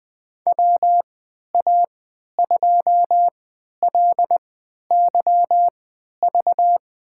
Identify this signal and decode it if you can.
Morse code, heard as a keyed tone, about 20 wpm, spaced as for 11 wpm, 700 Hz. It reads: WA2LYV
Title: USA Call Signs - 6 Characters - 3x Character Spacing - 20wpm